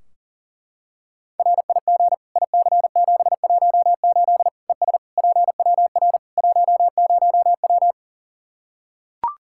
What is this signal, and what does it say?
Morse code, heard as a keyed tone, about 40 wpm, 700 Hz. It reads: RIG IC718 ES PWR 10W
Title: Practice QSO - Exchange 2 - Info - 40wpm